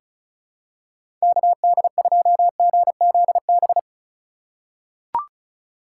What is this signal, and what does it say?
Morse code, heard as a keyed tone, about 34 wpm, 700 Hz. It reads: KD2GZB
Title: USA Call Signs - 6 Characters - 2x Character Spacing - 35wpm